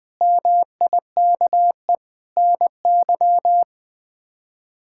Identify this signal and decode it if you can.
Morse code, heard as a keyed tone, about 21 wpm, 700 Hz. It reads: MIKE NY
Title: K1USN SST QSO Practice - 20wpm